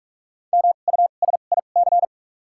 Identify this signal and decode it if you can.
Morse code, heard as a keyed tone, about 47 wpm, spaced as for 33 wpm, 700 Hz. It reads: MUSIC
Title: 2x Character Spacing - Top 500 Words - 45wpm